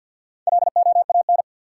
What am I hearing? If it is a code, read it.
Morse code, heard as a keyed tone, about 53 wpm, 700 Hz. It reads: LOAN
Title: Four Letter Words - 50wpm